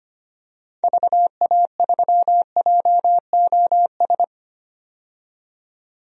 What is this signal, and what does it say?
Morse code, heard as a keyed tone, about 25 wpm, 690 Hz. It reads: VA3JOS
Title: DXCC-100 Call Signs - 6 Characters - 25wpm